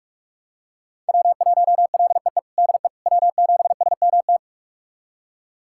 Morse code, heard as a keyed tone, about 46 wpm, 700 Hz. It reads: W1LEE DE W7SMT
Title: USA Callsign DE Callsign - 5 Characters - 45wpm